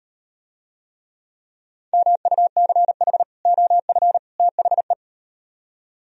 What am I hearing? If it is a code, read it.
Morse code, heard as a keyed tone, about 38 wpm, 700 Hz. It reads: MUCH OF THE